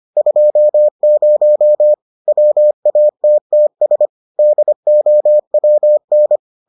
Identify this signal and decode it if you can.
Morse code, heard as a keyed tone, about 25 wpm, 600 Hz. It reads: 20 WATTS DOWN